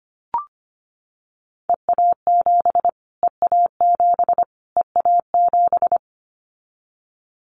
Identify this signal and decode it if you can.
Morse code, heard as a keyed tone, about 25 wpm, 700 Hz. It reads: EA7 EA7 EA7
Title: Callsign Prefixes for Top 25 DXCC - Repeated 3x - 25wpm